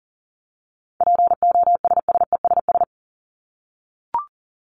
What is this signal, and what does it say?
Morse code, heard as a keyed tone, about 40 wpm, 700 Hz. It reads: POSSESS
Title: Top 1751-2000 Words 40wpm